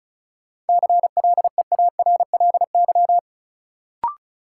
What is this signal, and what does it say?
Morse code, heard as a keyed tone, about 36 wpm, 700 Hz. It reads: CLEARLY
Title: Sentences from Top 1300 Words - New Word Drill - 35wpm